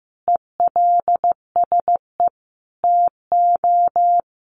Morse code, heard as a keyed tone, about 15 wpm, 700 Hz. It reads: ELSE TO
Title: Sentences from Top 1400 Words 15wpm